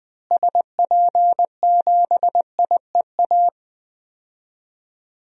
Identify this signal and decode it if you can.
Morse code, heard as a keyed tone, about 20 wpm, 700 Hz. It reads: SP7IEA
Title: DXCC-100 Call Signs - 6 Characters - Repeated 3x - 20wpm